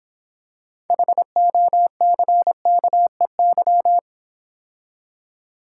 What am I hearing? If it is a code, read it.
Morse code, heard as a keyed tone, about 26 wpm, 700 Hz. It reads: HOCKEY